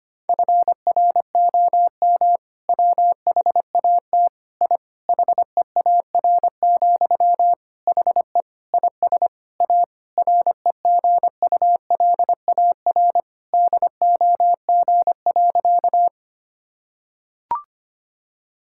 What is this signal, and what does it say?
Morse code, heard as a keyed tone, about 25 wpm, 700 Hz. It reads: FROM WHAT I HEAR, HE IS A REGULAR DOG.
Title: Sentences from Top 1000 Words - Review Part 6 - 25wpm